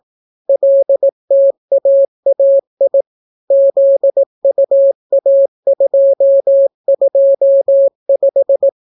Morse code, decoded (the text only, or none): LTAAI ZUA225